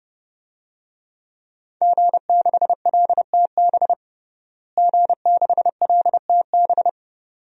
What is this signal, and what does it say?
Morse code, heard as a keyed tone, about 30 wpm, 700 Hz. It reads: G6LTB G6LTB
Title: DXCC-100 Call Signs - 5 Characters - Repeated 2x - 30wpm